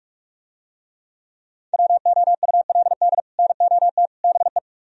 Morse code, heard as a keyed tone, about 44 wpm, 700 Hz. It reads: WOULD NOT BE